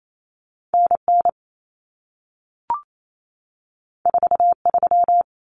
Morse code, heard as a keyed tone, about 28 wpm, 700 Hz. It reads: NN 43
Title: Common Cut Numbers - 2-Digit 28wpm